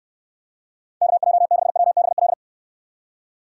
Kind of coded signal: Morse code

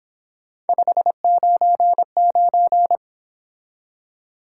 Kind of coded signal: Morse code

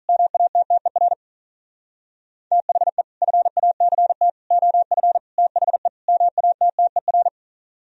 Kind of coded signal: Morse code

